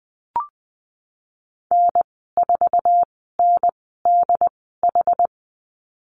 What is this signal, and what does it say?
Morse code, heard as a keyed tone, about 20 wpm, spaced as for 15 wpm, 700 Hz. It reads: N4NDH
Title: 186 Lesson 30g - Letter b - Call Signs 20wpm